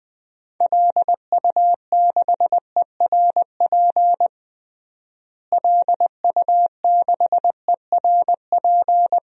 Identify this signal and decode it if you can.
Morse code, heard as a keyed tone, about 20 wpm, 690 Hz. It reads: LU6ERP LU6ERP
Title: DXCC-100 Call Signs - 6 Characters - Repeated 2x - 20wpm